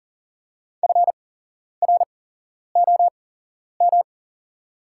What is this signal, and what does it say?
Morse code, heard as a keyed tone, about 44 wpm, spaced as for 11 wpm, 690 Hz. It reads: FROM
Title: 8x Character Spacing - Top 200 Words - 40wpm